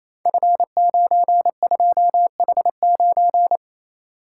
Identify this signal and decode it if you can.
Morse code, heard as a keyed tone, about 28 wpm, 700 Hz. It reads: F92H9
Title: Code Groups - 5 Letter:Numbers - 28wpm